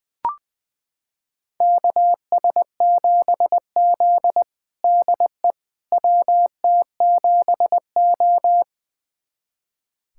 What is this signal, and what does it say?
Morse code, heard as a keyed tone, about 21 wpm, 700 Hz. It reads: KS7Z DE WT7O
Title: USA Callsign DE Callsign - 4 Characters - 20wpm